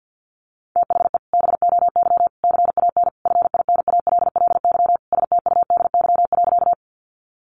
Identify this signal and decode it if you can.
Morse code, heard as a keyed tone, about 50 wpm, 700 Hz. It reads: THE BOY CAN FINALLY STUDY.
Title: Additional Easy Phrases 50wpm